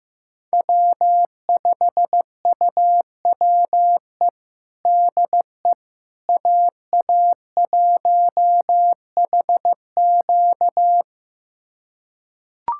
Morse code, heard as a keyed tone, about 15 wpm, 700 Hz. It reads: W5UWE DE AA1HQ
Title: USA Callsign DE Callsign - 5 Characters - 15wpm